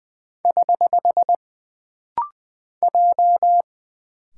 Morse code, heard as a keyed tone, about 20 wpm, 700 Hz. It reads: <HH> J
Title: Single Character (Letter:Number:Punctuation:Prosign) - Rapid-Fire - 20wpm